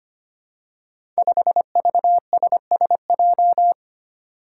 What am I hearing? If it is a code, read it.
Morse code, heard as a keyed tone, about 25 wpm, 700 Hz. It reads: 5VSSJ
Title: Five Challenging Letter-Number Combinations 25wpm